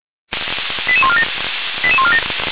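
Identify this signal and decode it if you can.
Five-tone selective call (ZVEI), heard as quick runs of five tones, tones of about 70 ms. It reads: 88147 88147
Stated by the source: ZVEI five-tone call received on SDR, from the feuerwehr-orlamuende.org fire brigade server.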